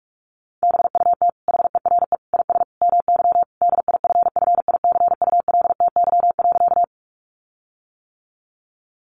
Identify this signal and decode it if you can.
Morse code, heard as a keyed tone, about 44 wpm, 700 Hz. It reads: BUT HERE IS MY DIFFICULTY.
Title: Sentences from Top 1500 Words 45wpm